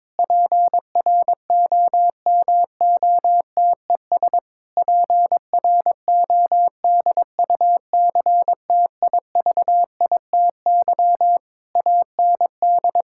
Morse code, heard as a keyed tone, about 22 wpm, 700 Hz. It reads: PROMOTES PRODUCTIVITY AND
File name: Difficult Sentences with Hints - 22wpm